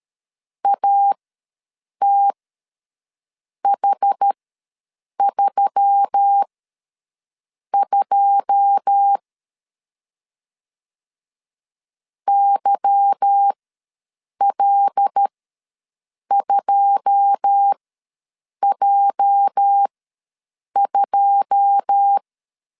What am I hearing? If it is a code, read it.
Morse code, heard as a keyed tone, about 13 wpm, spaced as for 6 wpm, 800 Hz. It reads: ATH32 YL2J2
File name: LEZIONI / 9 2 ws 12wpm 3406